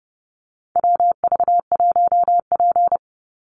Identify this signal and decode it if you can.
Morse code, heard as a keyed tone, about 30 wpm, 700 Hz. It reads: WV1P
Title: USA Call Signs 30wpm